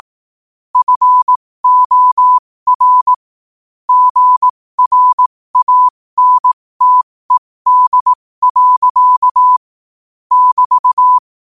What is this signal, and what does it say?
Morse code, heard as a keyed tone, about 18 wpm, spaced as for 15 wpm, 1000 Hz. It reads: FOR GRANTED. <BT>
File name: qotdfiles / 15wpm